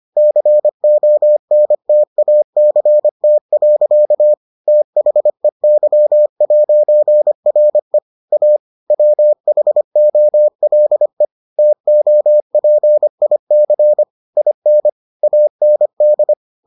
Morse code, heard as a keyed tone, about 25 wpm, 600 Hz. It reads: CONTACT. THEY'RE A WHOLE TOPIC IN AND